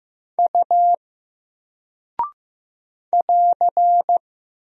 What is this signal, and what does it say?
Morse code, heard as a keyed tone, about 15 wpm, 700 Hz. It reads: U <AR>
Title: Single Character (Letter:Number:Punctuation:Prosign) 15wpm